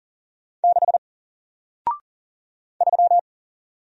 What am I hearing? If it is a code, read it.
Morse code, heard as a keyed tone, about 40 wpm, 700 Hz. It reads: 6 3
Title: Single Letter-Number Alt 40wpm